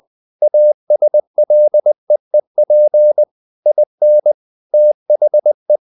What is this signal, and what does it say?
Morse code, heard as a keyed tone, about 20 wpm, 600 Hz. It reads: ASLEEP IN THE
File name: the old man and the sea / the old man and the sea part 1 20wpm